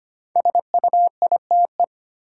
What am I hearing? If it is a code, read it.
Morse code, heard as a keyed tone, about 25 wpm, 700 Hz. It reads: SUITE
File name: 5 Letter Words - First 3 Letters Given - 25wpm